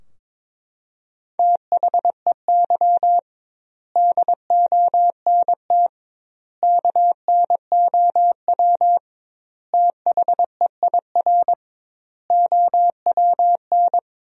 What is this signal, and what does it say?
Morse code, heard as a keyed tone, about 22 wpm, 700 Hz. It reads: THEY DONT KNOW THEIR OWN
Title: 2x Word-Spacing - Sentences from Top 300 Words - 22wpm